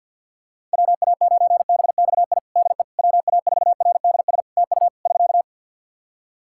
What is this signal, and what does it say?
Morse code, heard as a keyed tone, about 45 wpm, 700 Hz. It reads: WA9BXI DE WA4RDS TU <SK>
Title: Practice QSO - Exchange 4 - Ending - 50wpm